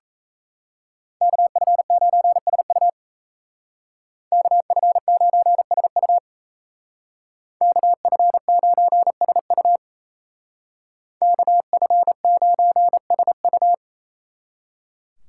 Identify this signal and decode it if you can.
Morse code, heard as a keyed tone, about 35 wpm, 700 Hz. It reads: KF9SU KF9SU KF9SU KF9SU